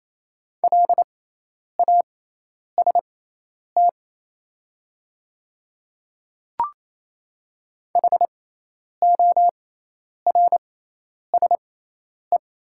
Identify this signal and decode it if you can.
Morse code, heard as a keyed tone, about 27 wpm, spaced as for 10 wpm, 700 Hz. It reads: LAST HORSE
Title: 6x Character Spacing - Top 300 Words - 28wpm